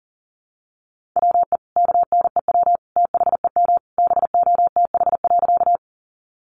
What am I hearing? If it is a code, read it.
Morse code, heard as a keyed tone, about 40 wpm, 700 Hz. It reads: WE KNEW THEM BOTH.
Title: Sentences from Top 400 Words 40wpm